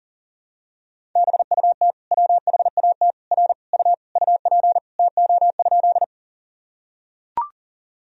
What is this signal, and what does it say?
Morse code, heard as a keyed tone, about 40 wpm, 700 Hz. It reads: BUT WHAT R U UP TO?
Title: Sentences from Top 100 Words and Abbreviations 40wpm